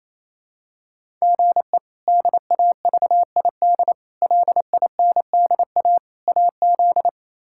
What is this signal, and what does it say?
Morse code, heard as a keyed tone, about 28 wpm, 700 Hz. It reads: GE DAVID LINDA AZ
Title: K1USN SST QSO Practice - 28wpm